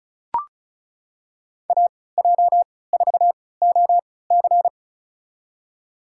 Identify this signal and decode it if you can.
Morse code, heard as a keyed tone, about 36 wpm, spaced as for 20 wpm, 690 Hz. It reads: AJ4OC
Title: USA Call Signs - 5 Characters - 3x Character Spacing - 35wpm